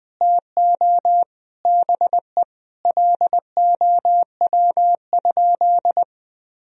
Morse code, heard as a keyed tone, about 20 wpm, 700 Hz. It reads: TO BE LOW?
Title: Sentences from Top 1000 Words - Review Part 1 - 20wpm